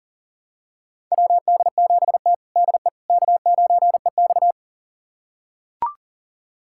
Morse code, heard as a keyed tone, about 43 wpm, 690 Hz. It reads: WD7T DE K9EX
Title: USA Callsign DE Callsign - 4 Characters - 40wpm